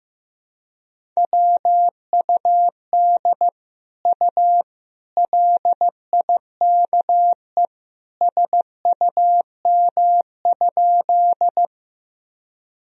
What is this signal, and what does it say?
Morse code, heard as a keyed tone, about 16 wpm, 700 Hz. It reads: WUD U LIKE SUM?
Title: Sentences from Top 100 Words and Abbreviations 15wpm